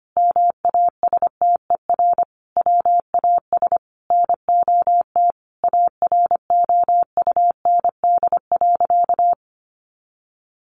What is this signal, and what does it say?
Morse code, heard as a keyed tone, about 25 wpm, 700 Hz. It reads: MASTER WAS NOT AROUND.
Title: Sentences from Top 1300 Words 25wpm